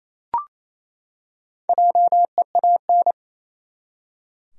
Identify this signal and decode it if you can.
Morse code, heard as a keyed tone, about 28 wpm, 700 Hz. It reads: JEAN
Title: USA Common Names - Max 4 Letters - 28wpm